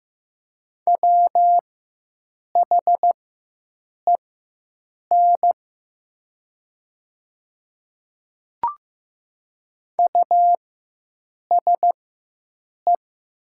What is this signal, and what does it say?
Morse code, heard as a keyed tone, about 15 wpm, spaced as for 7 wpm, 700 Hz. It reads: WHEN USE
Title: 4x Character Spacing - Top 100 Words - 15wpm